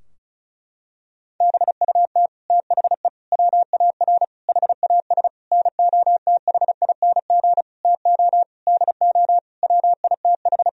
Morse code, heard as a keyed tone, about 36 wpm, 700 Hz. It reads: BUT THE WAR HAS NOTHING TO DO WITH
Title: Sentences from Top 1000 Words - Review Part 2 - 35wpm